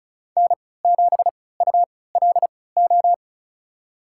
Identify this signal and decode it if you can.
Morse code, heard as a keyed tone, about 36 wpm, spaced as for 20 wpm, 700 Hz. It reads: N7ULO
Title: USA Call Signs - 5 Characters - 3x Character Spacing - 35wpm